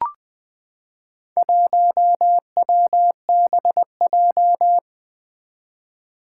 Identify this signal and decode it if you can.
Morse code, heard as a keyed tone, about 20 wpm, 700 Hz. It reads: 1WBJ
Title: Four Challenging Letter-Number Combinations 20wpm